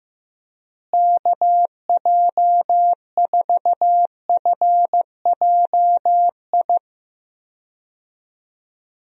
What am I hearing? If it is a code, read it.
Morse code, heard as a keyed tone, about 15 wpm, 700 Hz. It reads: KJ4FJI
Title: USA Call Signs - 6 Characters - Repeated 2x - 15wpm